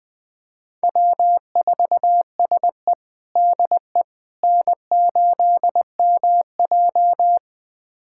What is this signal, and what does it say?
Morse code, heard as a keyed tone, about 20 wpm, 700 Hz. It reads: W4SE DE N8MJ
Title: USA Callsign DE Callsign - 4 Characters - 20wpm